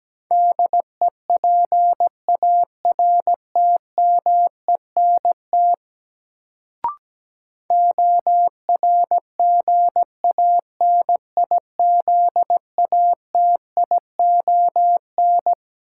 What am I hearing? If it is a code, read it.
Morse code, heard as a keyed tone, about 17 wpm, 700 Hz. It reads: DEPARTMENT ORGANIZATION
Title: Sentences from Top 1000 Words - Review Part 5 - New Word Drill - 17wpm